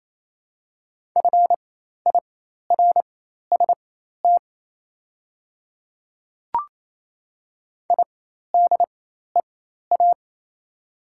Morse code, heard as a keyed tone, about 27 wpm, spaced as for 13 wpm, 690 Hz. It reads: FIRST IDEA
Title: 4x Character Spacing - Top 500 Words - 28wpm